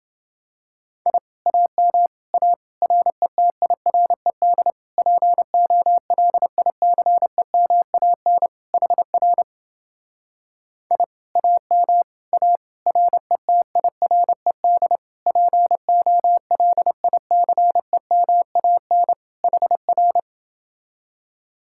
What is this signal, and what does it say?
Morse code, heard as a keyed tone, about 29 wpm, 700 Hz. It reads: I AM A RETIRED POLICEMAN HR I AM A RETIRED POLICEMAN HR